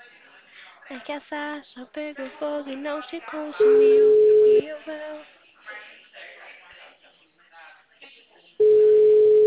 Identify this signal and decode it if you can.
Ringback tone (Europe, 425 Hz), with speech.
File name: out- -2024- - - .982